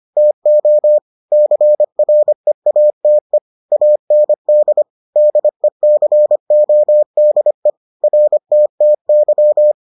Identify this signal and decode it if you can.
Morse code, heard as a keyed tone, about 25 wpm, 600 Hz. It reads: TO CREATE AND DECODE RTTY